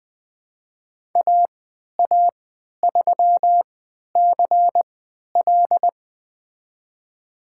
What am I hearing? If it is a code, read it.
Morse code, heard as a keyed tone, about 20 wpm, spaced as for 11 wpm, 700 Hz. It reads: AA3CL
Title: USA Call Signs - 5 Characters - 3x Character Spacing - 20wpm